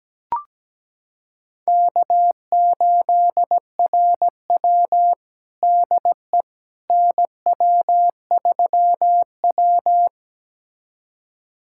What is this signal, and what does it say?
Morse code, heard as a keyed tone, about 17 wpm, 700 Hz. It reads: K8RW DE NW3W
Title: USA Callsign DE Callsign - 4 Characters - 17wpm